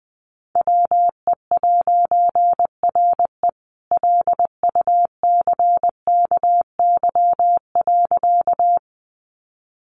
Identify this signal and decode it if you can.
Morse code, heard as a keyed tone, about 20 wpm, 700 Hz. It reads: WE'RE LUCKY.
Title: A Sound Of Thunder 20wpm